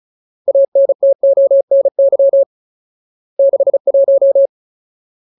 Morse code, heard as a keyed tone, about 34 wpm, 550 Hz. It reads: ANTONY 61
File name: Ultimate Word List / 35WPM Ultimate